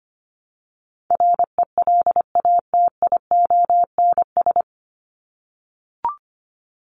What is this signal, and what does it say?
Morse code, heard as a keyed tone, about 25 wpm, 700 Hz. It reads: RELATIONS
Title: ES:S Suffix - Top 100 Words with Base Words 25wpm